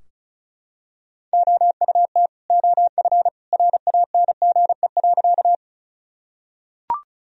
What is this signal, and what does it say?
Morse code, heard as a keyed tone, about 34 wpm, 700 Hz. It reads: OUT OF RANGE.
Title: Sentences from Top 1000 Words - Review Part 4 - 35wpm